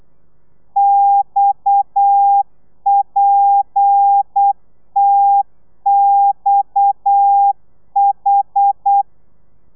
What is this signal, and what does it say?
Morse code, heard as a keyed tone, about 8 wpm, 800 Hz. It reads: XPTXH